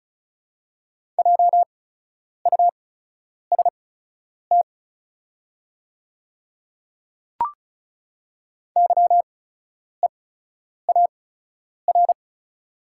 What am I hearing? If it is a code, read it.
Morse code, heard as a keyed tone, about 36 wpm, spaced as for 10 wpm, 700 Hz. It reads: JUST YEAR
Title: 8x Character Spacing - Top 200 Words - 35wpm